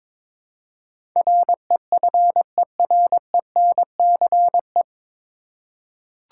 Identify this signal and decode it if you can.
Morse code, heard as a keyed tone, about 22 wpm, 700 Hz. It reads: REFERENCE
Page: Morse Code Ninja practice 22wpm